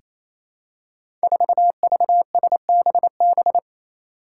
Morse code, heard as a keyed tone, about 28 wpm, 700 Hz. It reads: 4VSBB
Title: Five Challenging Letter-Number Combinations 28wpm